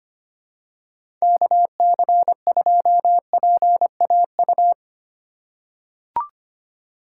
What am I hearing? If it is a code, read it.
Morse code, heard as a keyed tone, about 25 wpm, 700 Hz. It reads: KC2PAU
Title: USA Call Signs - 6 Characters - 3x Character Spacing - 25wpm